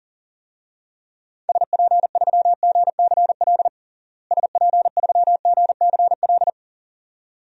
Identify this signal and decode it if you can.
Morse code, heard as a keyed tone, about 40 wpm, 700 Hz. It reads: SP3GCL SP3GCL
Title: DXCC-100 Call Signs - 6 Characters - Repeated 2x - 40wpm